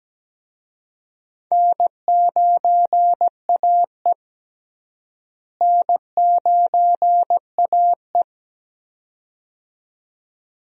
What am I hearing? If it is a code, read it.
Morse code, heard as a keyed tone, about 17 wpm, 700 Hz. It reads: N9AE N9AE
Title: USA Call Signs - 4 Characters - Repeated 2x - 17wpm